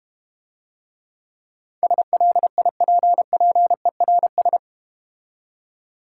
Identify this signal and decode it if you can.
Morse code, heard as a keyed tone, about 31 wpm, 700 Hz. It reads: SLIPPERS
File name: practice / 25wpm / Difficult Words - Speed-Racing with Hints - 25wpm